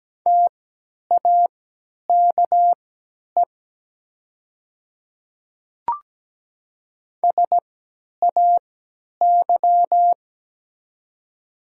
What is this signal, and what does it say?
Morse code, heard as a keyed tone, about 17 wpm, spaced as for 10 wpm, 700 Hz. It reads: TAKE SAY
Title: 3x Character Spacing - Top 500 Words - 17wpm